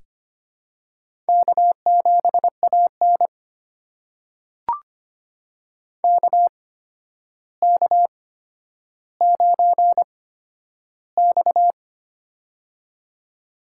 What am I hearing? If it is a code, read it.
Morse code, heard as a keyed tone, about 25 wpm, spaced as for 11 wpm, 700 Hz. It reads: K7AN KK9X
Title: USA Call Signs - 4 Characters - 8x Character Spacing - 25wpm